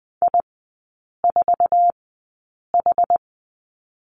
Morse code, heard as a keyed tone, about 20 wpm, spaced as for 8 wpm, 700 Hz. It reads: I4H
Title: 166 Lesson 27e - Number 7 - Three Characters 20wpm